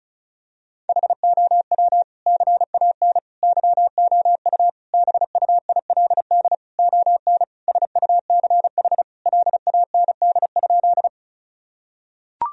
Morse code, heard as a keyed tone, about 34 wpm, 700 Hz. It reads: HOW CAN YOU BUILD ON SUCH LAND?
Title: Sentences from Top 1000 Words - Review Part 1 - 35wpm